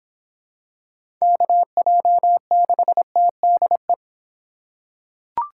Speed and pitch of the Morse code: 24 wpm, 700 Hz

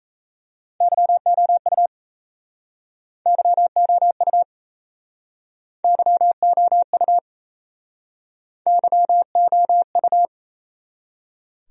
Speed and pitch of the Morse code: 39 wpm, 700 Hz

